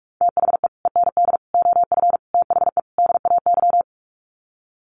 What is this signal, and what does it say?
Morse code, heard as a keyed tone, about 44 wpm, 700 Hz. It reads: THE END OF THE DAY
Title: Sets of 5 Words 45wpm